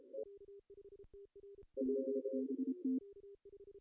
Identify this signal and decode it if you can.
Morse code, heard as a keyed tone, about 28 wpm, 260 Hz. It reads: BBT
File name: CW / cqww-cw-k5tr